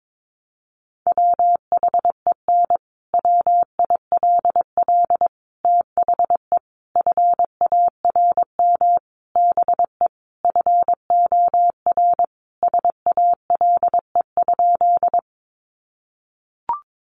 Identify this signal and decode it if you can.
Morse code, heard as a keyed tone, about 22 wpm, 700 Hz. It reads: WHEN WILL THE FARM BE FOR SALE?
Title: Sentences from Top 1400 Words 22wpm